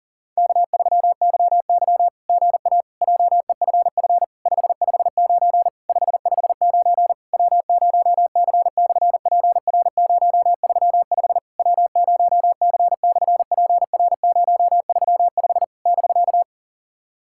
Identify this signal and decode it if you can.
Morse code, heard as a keyed tone, about 40 wpm, 700 Hz. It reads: K3YY GA JEFF 559 559 W0C/PR035 W0C/PR035 <BK>
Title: SOTA QSO Practice 40wpm